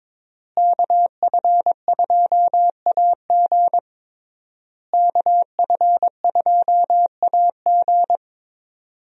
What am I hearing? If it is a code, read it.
Morse code, heard as a keyed tone, about 21 wpm, 700 Hz. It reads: KF2AG KF2AG
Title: USA Call Signs - 5 Characters - Repeated 3x - 22wpm